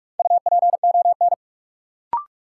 Morse code, heard as a keyed tone, about 44 wpm, 700 Hz. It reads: UPON
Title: Sentences from Top 1000 Words - Review Part 6 - New Word Drill - 45wpm